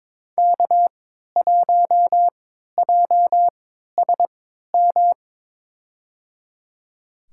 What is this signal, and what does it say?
Morse code, heard as a keyed tone, about 22 wpm, spaced as for 13 wpm, 700 Hz. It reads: K1JSM
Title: USA Call Signs - 3x Character Spacing - 22wpm